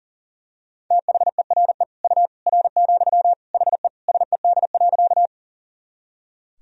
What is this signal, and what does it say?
Morse code, heard as a keyed tone, about 40 wpm, 700 Hz. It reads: THERE U R, HE SED.
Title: Sentences from Top 100 Words and Abbreviations 40wpm